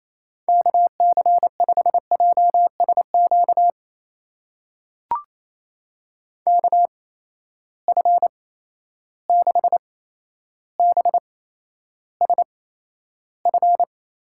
Morse code, heard as a keyed tone, about 28 wpm, spaced as for 12 wpm, 700 Hz. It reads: KC5JSQ KF6BSF
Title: USA Call Signs - 6 Characters - 8x Character Spacing - 28wpm